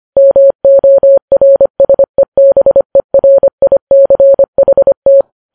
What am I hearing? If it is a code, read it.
Morse code, heard as a keyed tone, about 25 wpm, 550 Hz. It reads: MORSEBERICHT